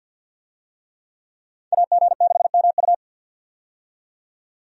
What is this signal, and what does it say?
Morse code, heard as a keyed tone, about 51 wpm, 700 Hz. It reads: AG6MU
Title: USA Call Signs - 5 Characters - 50wpm